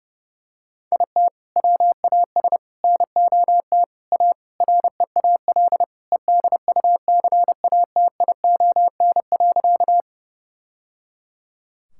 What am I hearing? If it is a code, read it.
Morse code, heard as a keyed tone, about 30 wpm, 700 Hz. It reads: IT WAS NOT A REAL EDUCATION.
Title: Sentences from Top 1000 Words - Review Part 6 - 30wpm